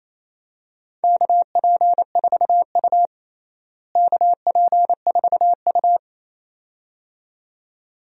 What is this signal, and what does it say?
Morse code, heard as a keyed tone, about 28 wpm, 700 Hz. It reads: KP4U KP4U
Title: DXCC-100 Call Signs - 4 Characters - Repeated 2x - 28wpm